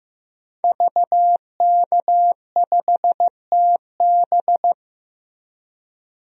Morse code, heard as a keyed tone, about 15 wpm, 700 Hz. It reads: VK5TB
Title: DXCC-100 Call Signs - 5 Characters - Repeated 3x - 15wpm